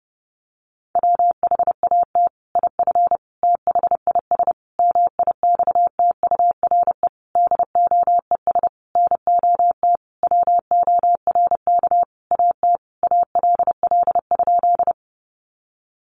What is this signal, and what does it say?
Morse code, heard as a keyed tone, about 30 wpm, 700 Hz. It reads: WHAT IF THIS MIXTURE DOES NOT WORK AT ALL?
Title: Sentences from Top 1500 Words 30wpm